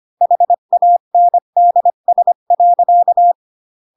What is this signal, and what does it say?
Morse code, heard as a keyed tone, about 25 wpm, 700 Hz. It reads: HANDS.
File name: Pinocchio / Pinocchio ch3 20 25 ik1hge